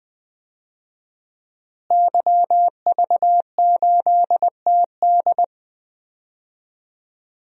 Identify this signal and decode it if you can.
Morse code, heard as a keyed tone, about 20 wpm, 700 Hz. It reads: YV8TD
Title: DXCC-100 Call Signs - 5 Characters - Repeated 2x - 20wpm